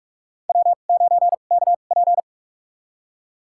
Morse code, heard as a keyed tone, about 41 wpm, spaced as for 31 wpm, 700 Hz. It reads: W9KP
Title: USA Call Signs - 2x Character Spacing - 45wpm